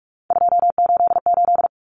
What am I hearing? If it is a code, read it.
Morse code, heard as a keyed tone, about 44 wpm, 700 Hz. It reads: 288
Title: Common Cut Numbers - 3-Digit 45wpm